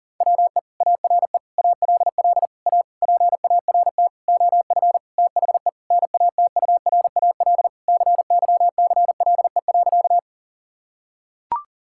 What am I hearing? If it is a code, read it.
Morse code, heard as a keyed tone, about 40 wpm, 700 Hz. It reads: WE ARE ALL A PART OF THE NATURAL CYCLE.
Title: Sentences from Top 1000 Words - Review Part 6 - 40wpm